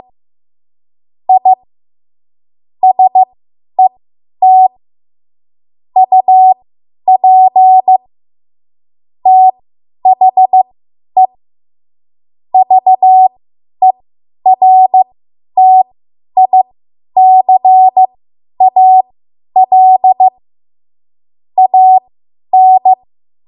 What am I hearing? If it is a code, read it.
Morse code, heard as a keyed tone, about 15 wpm, spaced as for 10 wpm, 750 Hz. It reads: I SET UP THE VERTICAL AN
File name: Archive / 10 WPM / 10WPM